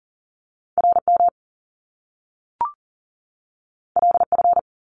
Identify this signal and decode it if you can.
Morse code, heard as a keyed tone, about 38 wpm, 700 Hz. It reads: RM LF
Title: Top Two Letter Combinations in Words 40wpm